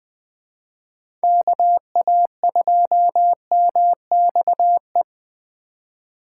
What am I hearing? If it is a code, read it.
Morse code, heard as a keyed tone, about 20 wpm, 700 Hz. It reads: KA2MXE
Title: USA Call Signs - 6 Characters - 4x Character Spacing - 20wpm